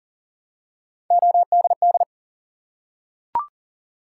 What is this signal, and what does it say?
Morse code, heard as a keyed tone, about 38 wpm, 700 Hz. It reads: ODD
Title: Three Letter Words - 40wpm